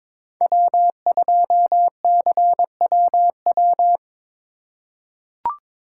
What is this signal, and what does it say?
Morse code, heard as a keyed tone, about 22 wpm, 700 Hz. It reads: W2CWW
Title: USA Call Signs - 5 Characters - 3x Character Spacing - 22wpm